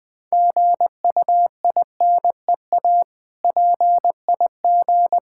Morse code, heard as a keyed tone, about 20 wpm, 690 Hz. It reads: GUINEA PIG
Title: Difficult Words with Hints - 20wpm